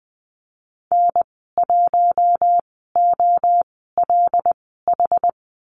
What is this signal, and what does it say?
Morse code, heard as a keyed tone, about 20 wpm, spaced as for 15 wpm, 700 Hz. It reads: N1OLH
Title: 076 Lesson 13g - Letter L - Call Signs 20wpm